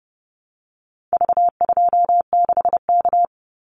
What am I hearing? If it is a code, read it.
Morse code, heard as a keyed tone, about 30 wpm, 700 Hz. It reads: V26K
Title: Call Signs Repeated 2x - 30wpm